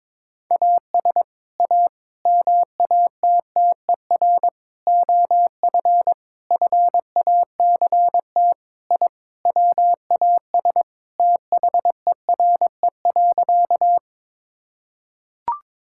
Morse code, heard as a keyed tone, about 22 wpm, 700 Hz. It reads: AS A MATTER OF FACT I WAS THERE.
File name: practice / 22wpm / Sentences with 4-Word Phrases - 22wpm